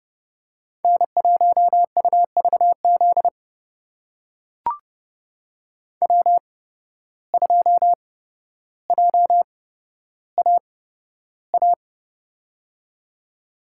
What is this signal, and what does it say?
Morse code, heard as a keyed tone, about 29 wpm, spaced as for 13 wpm, 700 Hz. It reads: N1UVZ W2JAA
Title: USA Call Signs - 5 Characters - 8x Character Spacing - 30wpm